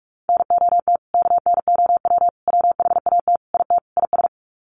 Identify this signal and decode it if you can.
Morse code, heard as a keyed tone, about 44 wpm, 700 Hz. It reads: NOT KNOW WHAT IT IS
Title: Sets of 5 Words Encore 45wpm